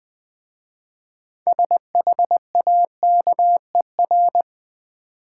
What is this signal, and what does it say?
Morse code, heard as a keyed tone, about 20 wpm, 700 Hz. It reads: SHAKER
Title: Easy Words with Hints - 20wpm